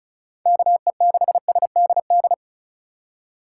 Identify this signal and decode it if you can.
Morse code, heard as a keyed tone, about 34 wpm, 700 Hz. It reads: KE6SDD
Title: USA Call Signs - 6 Characters - 4x Character Spacing - 35wpm